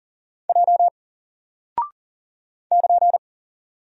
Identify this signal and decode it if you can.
Morse code, heard as a keyed tone, about 40 wpm, 700 Hz. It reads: J <KN>
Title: Single Character (Letter:Number:Punctuation:Prosign) Alt 40wpm